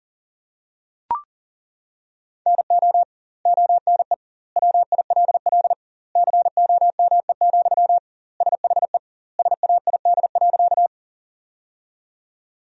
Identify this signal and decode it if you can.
Morse code, heard as a keyed tone, about 40 wpm, 700 Hz. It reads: NO ONE WILL COME, SHE SAID.
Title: 2x Word-Spacing - Sentences from Top 100 Words - 40wpm